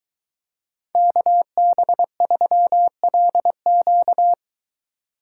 Morse code, heard as a keyed tone, about 22 wpm, 700 Hz. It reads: KB3LQ